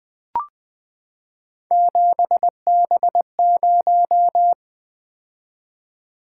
Morse code, heard as a keyed tone, about 20 wpm, 700 Hz. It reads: 7B0
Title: Three Letter-Number Combinations - 20wpm